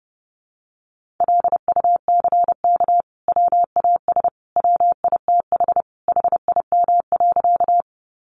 Morse code, heard as a keyed tone, about 30 wpm, 700 Hz. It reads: LUCK WAS WITH HIM.
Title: Sentences from Top 1500 Words 30wpm